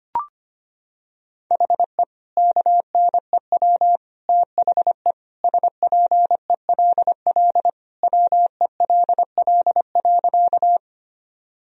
Morse code, heard as a keyed tone, about 25 wpm, 700 Hz. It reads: HE KNEW THE SPELL WELL.
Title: Sentences from Top 1000 Words - Review Part 1 - 25wpm